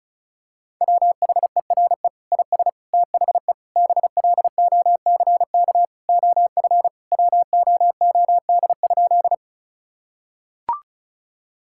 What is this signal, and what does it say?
Morse code, heard as a keyed tone, about 36 wpm, 690 Hz. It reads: WHERE IS THE BLOCK OF WOOD?
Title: Sentences from Top 1600 Words 35wpm